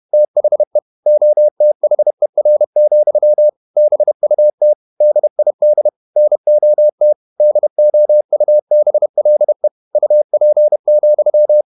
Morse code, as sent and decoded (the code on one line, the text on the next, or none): - .... / --- - .... .-. --..-- / -... ..- - / -.. .. -.. / -. --- - / -.. --- ..- -... .-.. / ..- .--. --..--
THE OTHER, BUT DID NOT DOUBLE UP,